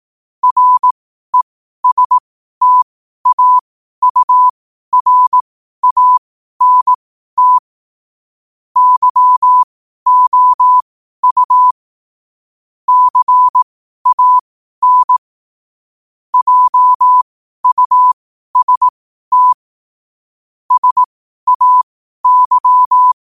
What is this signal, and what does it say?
Morse code, heard as a keyed tone, about 18 wpm, spaced as for 12 wpm, 1000 Hz. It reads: RESTAURANT YOU CAN JUST SAY
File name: qotdfiles / 12wpm